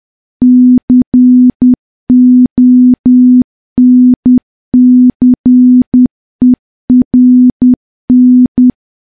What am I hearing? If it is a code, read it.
Morse code, heard as a keyed tone, about 10 wpm, 250 Hz. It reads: CONCERN
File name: morse / morse-training-10WPM